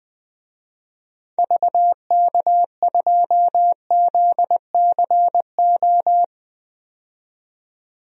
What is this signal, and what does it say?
Morse code, heard as a keyed tone, about 20 wpm, 700 Hz. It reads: VK2ZCO
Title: DXCC-100 Call Signs - 6 Characters - 20wpm